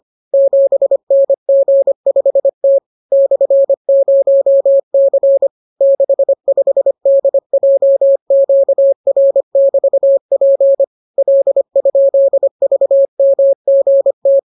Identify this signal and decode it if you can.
Morse code, heard as a keyed tone, about 25 wpm, 550 Hz. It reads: 7NG5T /0C 65DJQR<BT>P L?VMGT